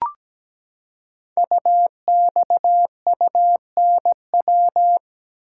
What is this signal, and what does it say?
Morse code, heard as a keyed tone, about 17 wpm, 700 Hz. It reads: UXUNW
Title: Code Groups - 5 Letters - 17wpm